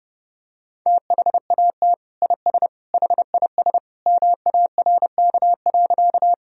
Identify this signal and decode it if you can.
Morse code, heard as a keyed tone, about 30 wpm, 700 Hz. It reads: THAT IS HIS MARK.
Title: Sentences from Top 1000 Words - Review Part 1 - 30wpm